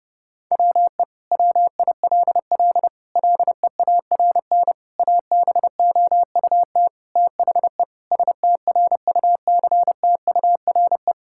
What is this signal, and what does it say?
Morse code, heard as a keyed tone, about 30 wpm, 700 Hz. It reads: WE WILL LEARN ABOUT THE STRUCTURE
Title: Sentences from Top 1000 Words - Review Part 6 - 30wpm